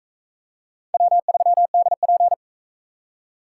Morse code, heard as a keyed tone, about 38 wpm, 700 Hz. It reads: W3DP